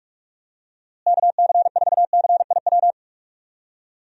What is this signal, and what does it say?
Morse code, heard as a keyed tone, about 46 wpm, 700 Hz. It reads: KC4CIW
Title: USA Call Signs - 6 Characters - Repeated 2x - 45wpm